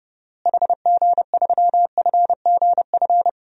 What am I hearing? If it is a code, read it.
Morse code, heard as a keyed tone, about 30 wpm, 700 Hz. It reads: HG3FGF
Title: DXCC-100 Call Signs - 6 Characters - Repeated 3x - 30wpm